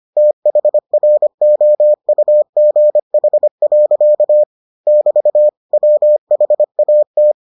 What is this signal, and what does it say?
Morse code, heard as a keyed tone, about 25 wpm, 600 Hz. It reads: THROUGH. <BT> WHAT